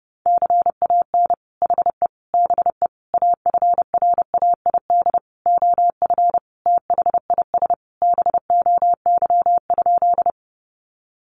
Morse code, heard as a keyed tone, about 30 wpm, 700 Hz. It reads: CAN HE BE AFRAID OF THIS BOY?
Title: Sentences from Top 1500 Words 30wpm